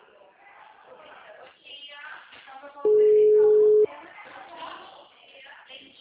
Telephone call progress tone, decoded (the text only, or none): ringback tone